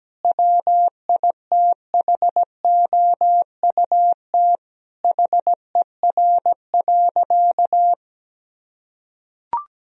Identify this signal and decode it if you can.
Morse code, heard as a keyed tone, about 17 wpm, 700 Hz. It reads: WITHOUT HER.
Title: Sentences from Top 1000 Words - Review Part 3 - 17wpm